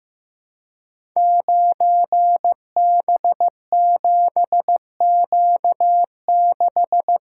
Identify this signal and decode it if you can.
Morse code, heard as a keyed tone, about 15 wpm, 700 Hz. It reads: 9B7Q6
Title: Code Groups - 5 Letter:Numbers - 15wpm